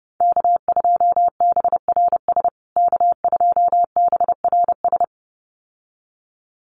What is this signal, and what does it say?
Morse code, heard as a keyed tone, about 30 wpm, 700 Hz. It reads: K2BRS K2BRS
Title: USA Call Signs Repeated 2x 30wpm